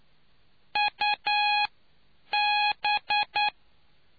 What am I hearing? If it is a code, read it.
Morse code, heard as a keyed tone, about 9 wpm, spaced as for 7 wpm, 830 Hz. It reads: UB